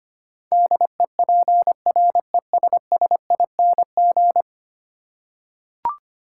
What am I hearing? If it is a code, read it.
Morse code, heard as a keyed tone, about 25 wpm, 700 Hz. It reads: DEPRESSING
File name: 10 Letter Words - First 3 Letters Given - 25wpm